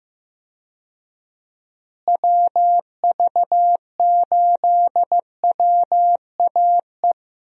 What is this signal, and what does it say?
Morse code, heard as a keyed tone, about 15 wpm, 700 Hz. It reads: WV8WAE
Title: USA Call Signs - 6 Characters - Repeated 2x - 15wpm